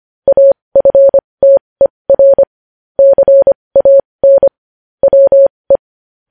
Morse code, heard as a keyed tone, about 25 wpm, spaced as for 20 wpm, 550 Hz. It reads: AFTER CAN WE